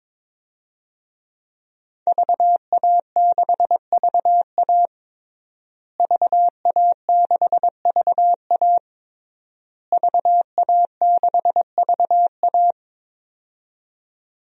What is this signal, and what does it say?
Morse code, heard as a keyed tone, about 22 wpm, 700 Hz. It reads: VA6VA VA6VA VA6VA
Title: DXCC-100 Call Signs - 5 Characters - Repeated 3x - 22wpm